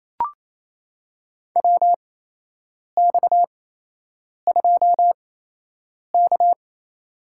Morse code, heard as a keyed tone, about 28 wpm, spaced as for 8 wpm, 700 Hz. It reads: WX2K
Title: USA Call Signs - 4 Characters - 8x Character Spacing - 28wpm